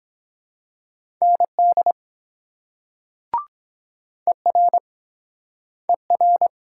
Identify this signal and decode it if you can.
Morse code, heard as a keyed tone, about 23 wpm, 700 Hz. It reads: ND ER ER